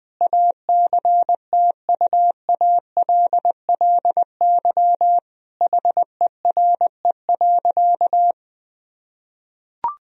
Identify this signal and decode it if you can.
Morse code, heard as a keyed tone, about 20 wpm, 700 Hz. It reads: ACTUALLY HERE.
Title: Sentences from Top 1000 Words - Review Part 4 - 20wpm